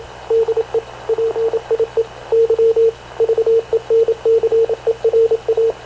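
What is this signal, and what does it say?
Morse code, heard as a keyed tone, about 27 wpm, 450 Hz, with radio noise. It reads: DE PIE Y VENCERA
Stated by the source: Havana Radio / CLA sending Spanish press (PX) in Morse, recorded by Radio Officer Paul Dumesnil onboard CNAV QUEST/CZDO.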